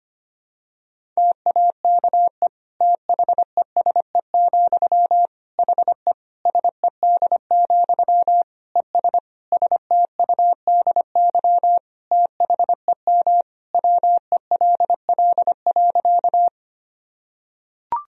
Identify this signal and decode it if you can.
Morse code, heard as a keyed tone, about 25 wpm, 700 Hz. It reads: TAKE THESE, HE SED, ES STUDY THEM WELL.
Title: Sentences from Top 300 Words and Abbreviations 25wpm